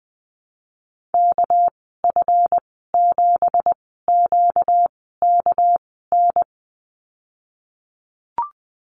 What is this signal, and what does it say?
Morse code, heard as a keyed tone, about 20 wpm, spaced as for 15 wpm, 700 Hz. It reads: KF7QKN
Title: 245 Lesson 39g - Letter z - Call Signs 20wpm